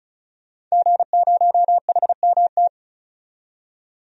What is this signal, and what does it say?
Morse code, heard as a keyed tone, about 36 wpm, 700 Hz. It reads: G0HMT
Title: DXCC-100 Call Signs - 5 Characters - 35wpm